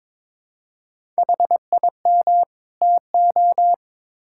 Morse code, heard as a keyed tone, about 22 wpm, 700 Hz. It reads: HIM TO
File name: Build A Phrase - 2-Word Phrases - 22wpm